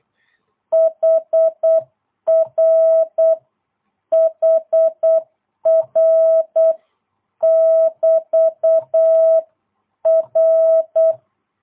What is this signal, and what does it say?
Morse code, heard as a keyed tone, about 8 wpm, 660 Hz. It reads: HR HR <BT> R
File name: jh8cbh / 2022cw